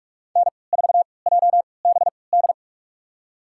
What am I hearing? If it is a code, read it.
Morse code, heard as a keyed tone, about 47 wpm, spaced as for 26 wpm, 700 Hz. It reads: N4JBD
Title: USA Call Signs - 5 Characters - 3x Character Spacing - 45wpm